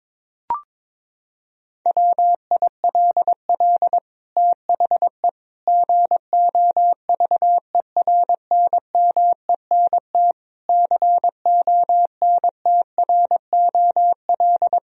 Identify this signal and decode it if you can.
Morse code, heard as a keyed tone, about 22 wpm, 700 Hz. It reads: WILL THE GOVERNMENT CONTROL
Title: Sentences from Top 1000 Words - Review Part 3 - 22wpm